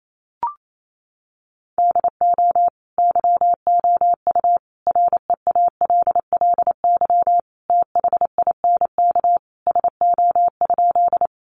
Morse code, heard as a keyed tone, about 28 wpm, 700 Hz. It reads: DO YOU REALLY THINK SO?
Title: Sentences from Top 600 Words 28wpm